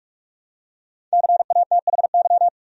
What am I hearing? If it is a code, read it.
Morse code, heard as a keyed tone, about 46 wpm, 700 Hz. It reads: CATHY
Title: USA Common Names - Max 5 Letters - 45wpm